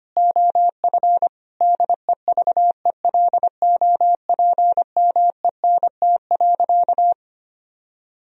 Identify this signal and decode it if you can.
Morse code, heard as a keyed tone, about 25 wpm, 700 Hz. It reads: OF DEVELOPMENT.
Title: Sentences from Top 1000 Words - Review Part 4 - 25wpm